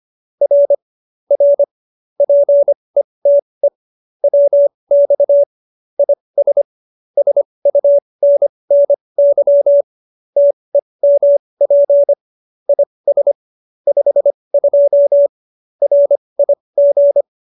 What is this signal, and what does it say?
Morse code, heard as a keyed tone, about 25 wpm, spaced as for 20 wpm, 570 Hz. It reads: R R PETE WX IS SUNNY TEMP IS 52 RIG